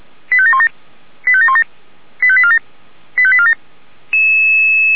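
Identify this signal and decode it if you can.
Five-tone selective call (ZVEI), heard as quick runs of five tones, tones of about 70 ms. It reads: 85717 85717 85847 85847
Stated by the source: ZVEI five-tone call received on SDR.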